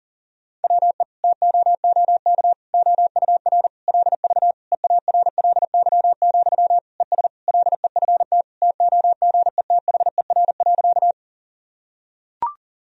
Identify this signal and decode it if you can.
Morse code, heard as a keyed tone, about 40 wpm, 700 Hz. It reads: WE TOOK OUR LV EARLY, ES LEFT TOGETHER.
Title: Sentences from Top 400 Words and Abbreviations 40wpm